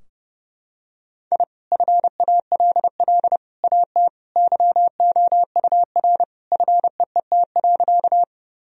Morse code, heard as a keyed tone, about 30 wpm, 700 Hz. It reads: I FALL AT YOUR FEET.
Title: Sentences from Top 1000 Words - Review Part 2 - 30wpm